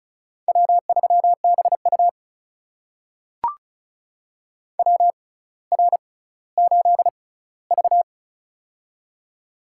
Morse code, heard as a keyed tone, about 36 wpm, spaced as for 18 wpm, 700 Hz. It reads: W3BU WR8V
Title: USA Call Signs - 4 Characters - 6x Character Spacing - 35wpm